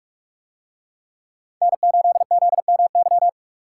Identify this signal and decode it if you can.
Morse code, heard as a keyed tone, about 44 wpm, 700 Hz. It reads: N8ZMY